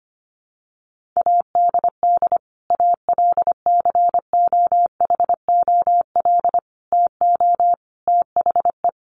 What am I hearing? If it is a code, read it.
Morse code, heard as a keyed tone, about 25 wpm, 700 Hz. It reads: ADD ALCOHOL TO THE
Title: Sentences from Top 1500 Words 25wpm